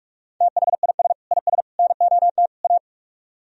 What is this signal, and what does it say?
Morse code, heard as a keyed tone, about 44 wpm, 700 Hz. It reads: THIS IS NOT A